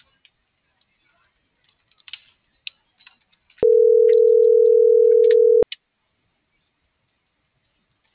Ringback tone (North America), 440 + 480 Hz.